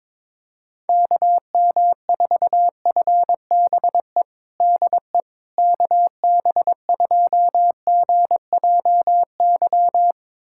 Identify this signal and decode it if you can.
Morse code, heard as a keyed tone, about 22 wpm, 700 Hz. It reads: KM4FBE DE KB2GJY
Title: USA Callsign DE Callsign - 6 Characters - 22wpm